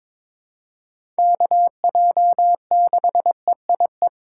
Morse code, heard as a keyed tone, about 22 wpm, 700 Hz. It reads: KJ6EIE
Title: USA Call Signs - 2x Character Spacing - 22wpm